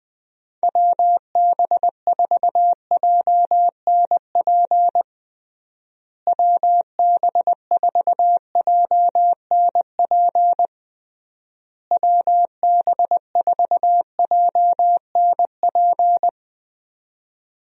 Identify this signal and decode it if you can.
Morse code, heard as a keyed tone, about 20 wpm, 700 Hz. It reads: WB4JNP WB4JNP WB4JNP
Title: USA Call Signs - 6 Characters - Repeated 3x - 20wpm